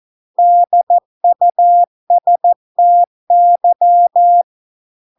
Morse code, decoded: DUSTY

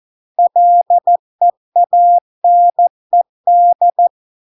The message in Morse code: LEANED

